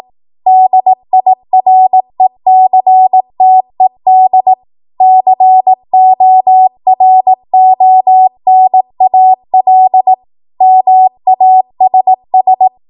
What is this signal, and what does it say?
Morse code, heard as a keyed tone, about 18 wpm, 750 Hz. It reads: DIRECTED CORONAL MASS
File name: Morse / 18WPM